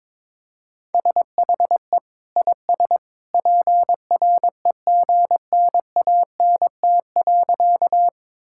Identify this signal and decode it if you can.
Morse code, heard as a keyed tone, about 22 wpm, 700 Hz. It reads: SHE IS PREGNANT.
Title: Sentences from Top 1000 Words - Review Part 6 - 22wpm